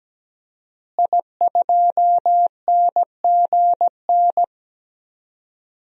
Morse code, heard as a keyed tone, about 17 wpm, 700 Hz. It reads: I2NGN
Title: DXCC-100 Call Signs - 5 Characters - Repeated 2x - 17wpm